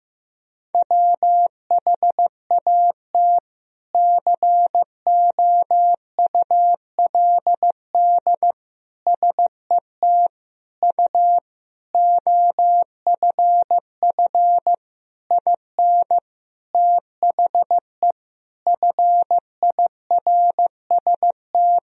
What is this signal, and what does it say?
Morse code, heard as a keyed tone, about 16 wpm, 700 Hz. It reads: WHAT COULD SET U OFF IN THE FIRST
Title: Sentences from Top 200 Words and Abbreviations 15wpm